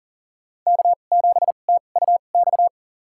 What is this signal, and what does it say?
Morse code, heard as a keyed tone, about 40 wpm, spaced as for 29 wpm, 700 Hz. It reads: K7TUX
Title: USA Call Signs - 5 Characters - 2x Character Spacing - 40wpm